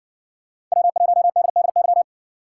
Morse code, heard as a keyed tone, about 60 wpm, 700 Hz. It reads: W1DDY